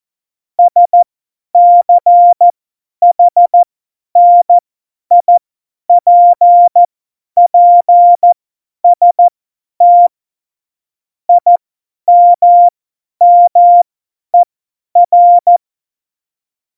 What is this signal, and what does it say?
Morse code, heard as a keyed tone, about 14 wpm, spaced as for 10 wpm, 700 Hz. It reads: SCHNIPPST IMMER
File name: Morsekurs / Witze / Witz 09. Januar 2018, 50 F70